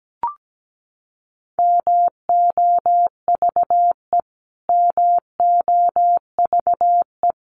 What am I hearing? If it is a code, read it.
Morse code, heard as a keyed tone, about 17 wpm, 700 Hz. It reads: MOVE MOVE
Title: Top 300 Words Repeated 17wpm